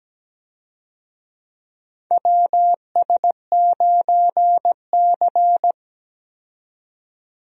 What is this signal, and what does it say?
Morse code, heard as a keyed tone, about 17 wpm, 700 Hz. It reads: WS9C
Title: USA Call Signs - 4 Characters - 17wpm